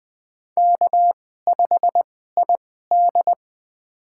Morse code, heard as a keyed tone, about 20 wpm, spaced as for 15 wpm, 700 Hz. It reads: K5ID
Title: USA Call Signs - 4 Characters - 2x Character Spacing - 20wpm